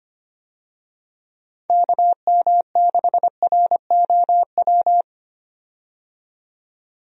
Morse code, heard as a keyed tone, about 25 wpm, 700 Hz. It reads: KM6ROW
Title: USA Call Signs - 6 Characters - 25wpm